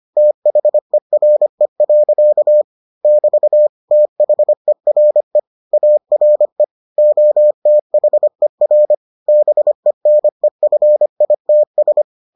Morse code, heard as a keyed tone, about 25 wpm, 600 Hz. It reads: THERE. <BT> THERE ARE OTHER BENEFITS